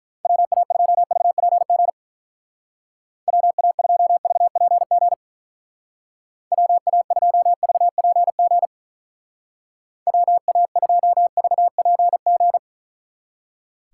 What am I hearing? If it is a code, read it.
Morse code, heard as a keyed tone, about 44 wpm, 700 Hz. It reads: WA2VPG WA2VPG WA2VPG WA2VPG